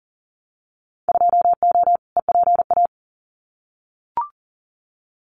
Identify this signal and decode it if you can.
Morse code, heard as a keyed tone, about 40 wpm, 700 Hz. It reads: 2O EPA
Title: Winter Field Day Exchanges Encore 40wpm